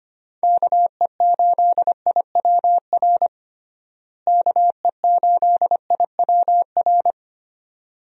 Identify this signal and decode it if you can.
Morse code, heard as a keyed tone, about 25 wpm, 700 Hz. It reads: KE8IWR KE8IWR
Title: USA Call Signs - 6 Characters - Repeated 2x - 25wpm